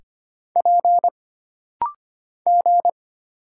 Morse code, heard as a keyed tone, about 25 wpm, 700 Hz. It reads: P G
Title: Single Letters - Rapid-Fire 25wpm